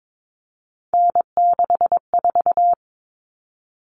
Morse code, heard as a keyed tone, about 22 wpm, 700 Hz. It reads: N64
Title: Common Cut Numbers - 3-Digit 22wpm